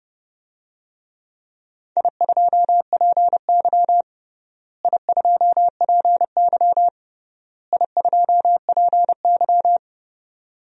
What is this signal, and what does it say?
Morse code, heard as a keyed tone, about 30 wpm, 700 Hz. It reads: I2PY I2PY I2PY
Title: DXCC-100 Call Signs - 4 Characters - Repeated 3x - 30wpm